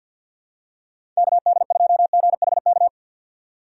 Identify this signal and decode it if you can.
Morse code, heard as a keyed tone, about 53 wpm, 700 Hz. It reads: K D 2 G H K
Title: USA Call Signs - 6 Characters - 8x Character Spacing - 50wpm